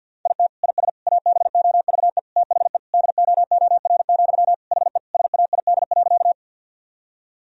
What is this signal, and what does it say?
Morse code, heard as a keyed tone, about 51 wpm, 700 Hz. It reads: IT IS ABOVE THE DOOR, HE SAID.
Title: Sentences from Top 1000 Words - Review Part 2 - 50wpm